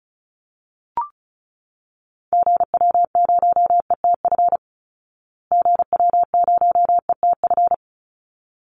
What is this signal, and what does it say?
Morse code, heard as a keyed tone, about 34 wpm, 700 Hz. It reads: GW0ETF GW0ETF
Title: Call Signs Repeated 3x - 35wpm